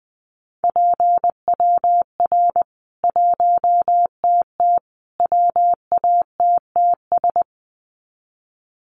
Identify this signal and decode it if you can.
Morse code, heard as a keyed tone, about 20 wpm, 700 Hz. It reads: PWR 1TT WATTS
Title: Common QSO Phrases 20wpm